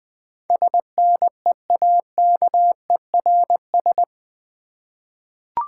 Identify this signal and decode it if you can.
Morse code, heard as a keyed tone, about 20 wpm, 690 Hz. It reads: SNEAKERS
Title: Easy Words with Hints - 20wpm